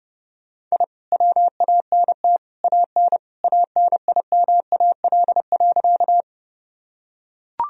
Morse code, heard as a keyed tone, about 32 wpm, 700 Hz. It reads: I WANT AN ANIMAL.
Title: Sentences from Top 1000 Words - Review Part 1 - 30wpm